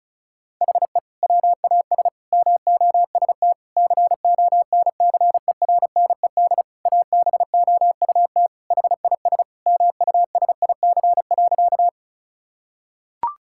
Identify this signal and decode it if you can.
Morse code, heard as a keyed tone, about 34 wpm, 690 Hz. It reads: HE WAS MOST CONCERNED ABOUT HIS MUSIC.
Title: Sentences from Top 1000 Words - Review Part 6 - 35wpm